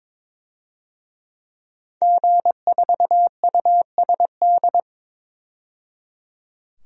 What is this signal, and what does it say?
Morse code, heard as a keyed tone, about 22 wpm, 700 Hz. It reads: G4USD
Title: DXCC-100 Call Signs - 5 Characters - 22wpm